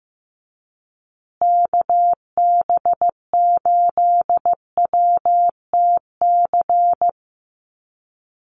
Morse code, heard as a keyed tone, about 15 wpm, 700 Hz. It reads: KB8WTC
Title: USA Call Signs Repeated 2x 15wpm